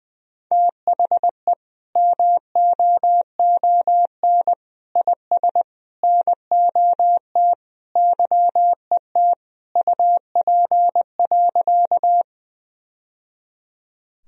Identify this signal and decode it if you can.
Morse code, heard as a keyed tone, about 20 wpm, 700 Hz. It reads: THE MOON IS NOT YET UP.
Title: Sentences from Top 1000 Words - Review Part 3 - 20wpm